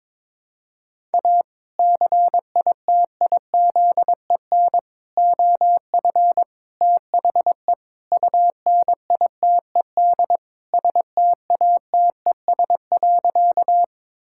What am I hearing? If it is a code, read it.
Morse code, heard as a keyed tone, about 22 wpm, 700 Hz. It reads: A CITIZEN OF THE UNITED STATES.
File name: practice / 22wpm / Sentences with 3-Word Phrases - 22wpm